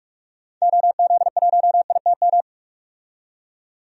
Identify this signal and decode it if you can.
Morse code, heard as a keyed tone, about 41 wpm, 700 Hz. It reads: OZ1ITM
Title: DXCC-100 Call Signs - 6 Characters - 45wpm